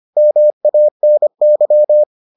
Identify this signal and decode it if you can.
Morse code, heard as a keyed tone, about 25 wpm, 600 Hz. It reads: MANY